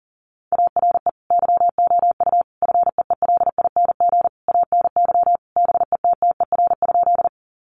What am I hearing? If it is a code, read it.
Morse code, heard as a keyed tone, about 40 wpm, 700 Hz. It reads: ARE YOU FEELING ANY BETTER?
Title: Sentences from Top 1200 Words 40wpm